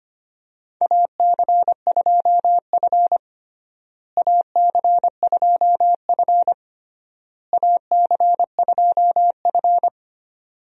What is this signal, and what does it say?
Morse code, heard as a keyed tone, about 25 wpm, 700 Hz. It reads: AC2F AC2F AC2F
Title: USA Call Signs - 4 Characters - Repeated 3x - 25wpm